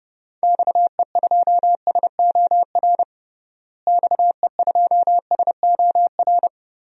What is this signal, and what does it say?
Morse code, heard as a keyed tone, about 29 wpm, 700 Hz. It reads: XE2SOR XE2SOR
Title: DXCC-100 Call Signs - 6 Characters - Repeated 2x - 30wpm